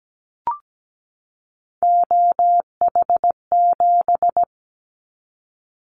Morse code, heard as a keyed tone, about 17 wpm, 700 Hz. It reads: OH7
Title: Callsign Prefixes for Top 50 DXCC - 17wpm